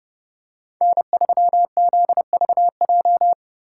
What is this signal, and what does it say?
Morse code, heard as a keyed tone, about 30 wpm, 700 Hz. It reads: N3ZVJ